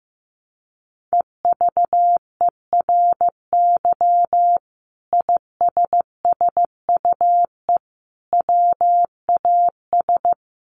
Morse code, heard as a keyed tone, about 15 wpm, 700 Hz. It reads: EVERY ISSUE WAS
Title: Sentences from Top 1400 Words 15wpm